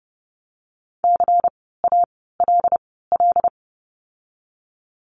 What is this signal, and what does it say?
Morse code, heard as a keyed tone, about 30 wpm, spaced as for 17 wpm, 700 Hz. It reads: CALL
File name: course / 30:10wpm / 108 Lesson 18f - Letter m - Words 30wpm